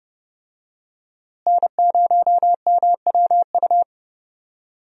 Morse code, heard as a keyed tone, about 30 wpm, 700 Hz. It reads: N0MWU
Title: USA Call Signs - 5 Characters - Repeated 3x - 30wpm